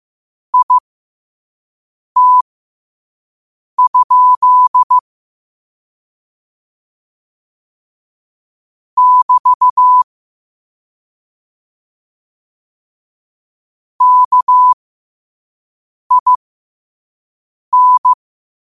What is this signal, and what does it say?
Morse code, heard as a keyed tone, about 15 wpm, spaced as for 5 wpm, 1000 Hz. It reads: IT? <BT> KIN
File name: qotdfiles / 5wpm